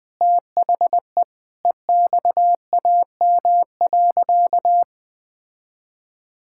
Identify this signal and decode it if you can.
Morse code, heard as a keyed tone, about 20 wpm, 700 Hz. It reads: THE EXAM.
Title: Sentences from Top 1000 Words - Review Part 5 - 20wpm